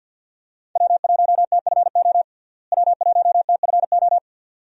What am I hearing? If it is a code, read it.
Morse code, heard as a keyed tone, about 51 wpm, 700 Hz. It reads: W1TFO W1TFO
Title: USA Call Signs - 5 Characters - Repeated 2x - 50wpm